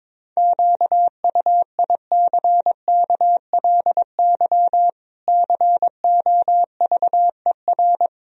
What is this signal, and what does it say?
Morse code, heard as a keyed tone, about 22 wpm, 700 Hz. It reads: QUICKLY COVER
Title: Sentences from Top 1000 Words - Review Part 1 - 22wpm